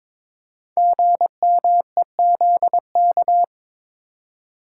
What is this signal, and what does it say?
Morse code, heard as a keyed tone, about 22 wpm, 700 Hz. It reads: GMEZK
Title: Code Groups - 5 Letter:Numbers - 22wpm